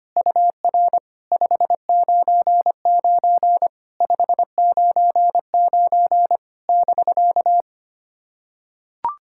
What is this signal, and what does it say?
Morse code, heard as a keyed tone, about 25 wpm, 700 Hz. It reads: UR 599 599 <BK>
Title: SOTA QSO Practice 25wpm